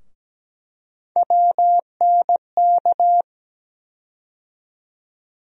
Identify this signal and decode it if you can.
Morse code, heard as a keyed tone, about 17 wpm, 700 Hz. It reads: WNK
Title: USA Call Sign Postfixes - 3 Letters 17wpm